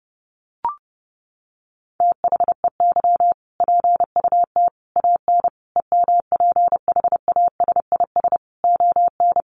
Morse code, heard as a keyed tone, about 30 wpm, 700 Hz. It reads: THEY PUT AN EMPHASIS ON
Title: Sentences from Top 1400 Words 30wpm